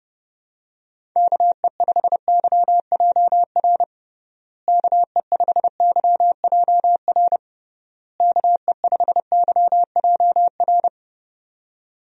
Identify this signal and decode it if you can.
Morse code, heard as a keyed tone, about 32 wpm, 700 Hz. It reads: KE5YJR KE5YJR KE5YJR
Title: USA Call Signs - 6 Characters - Repeated 3x - 30wpm